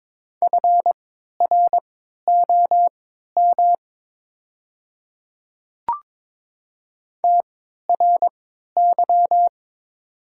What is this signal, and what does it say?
Morse code, heard as a keyed tone, about 22 wpm, spaced as for 13 wpm, 700 Hz. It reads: FROM TRY
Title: 3x Character Spacing - Top 200 Words - 22wpm